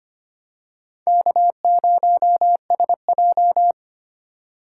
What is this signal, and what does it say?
Morse code, heard as a keyed tone, about 25 wpm, 700 Hz. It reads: K0SJ
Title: USA Call Signs - 8x Character Spacing - 25wpm